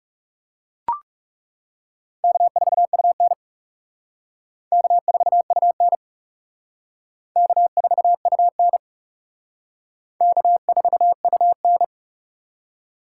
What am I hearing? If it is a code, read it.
Morse code, heard as a keyed tone, about 40 wpm, 700 Hz. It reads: K4UN K4UN K4UN K4UN